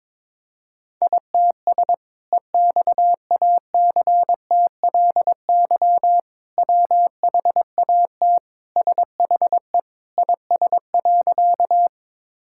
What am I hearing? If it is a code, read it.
Morse code, heard as a keyed tone, about 22 wpm, 700 Hz. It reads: ITS EXACTLY WHAT SHE IS.
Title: Sentences from Top 1000 Words - Review Part 6 - 22wpm